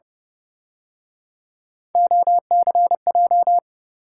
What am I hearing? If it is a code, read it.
Morse code, heard as a keyed tone, about 30 wpm, 700 Hz. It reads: OCJ
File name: Speed-Racing - Three Letter-Number Combinations - 25wpm